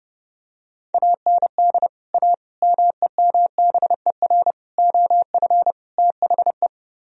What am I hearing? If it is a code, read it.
Morse code, heard as a keyed tone, about 30 wpm, 700 Hz. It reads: AND A MEMBER OF THE